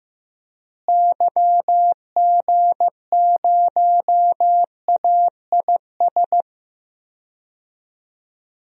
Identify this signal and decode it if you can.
Morse code, heard as a keyed tone, about 15 wpm, 700 Hz. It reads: YG0AIS
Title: DXCC-100 Call Signs - 6 Characters - Repeated 3x - 15wpm